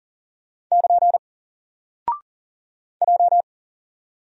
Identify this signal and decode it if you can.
Morse code, heard as a keyed tone, about 40 wpm, 700 Hz. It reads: <KN> J
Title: Single Character (Letter:Number:Punctuation:Prosign) 40wpm